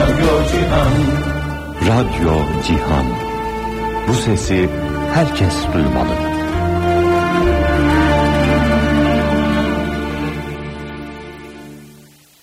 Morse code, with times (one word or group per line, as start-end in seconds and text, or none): none